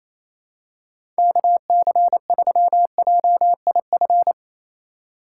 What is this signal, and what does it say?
Morse code, heard as a keyed tone, about 28 wpm, 700 Hz. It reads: KC3JIF
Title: USA Call Signs - 6 Characters - 3x Character Spacing - 28wpm